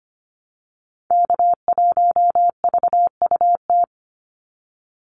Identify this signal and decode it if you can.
Morse code, heard as a keyed tone, about 25 wpm, 700 Hz. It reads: K1VUT
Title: Call Signs - 25wpm